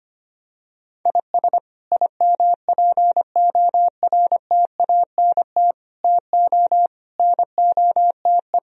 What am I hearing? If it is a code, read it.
Morse code, heard as a keyed tone, about 25 wpm, 700 Hz. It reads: IS IMPORTANT TO NOTE
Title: Speed-Racing - Sets of 5 Words 25wpm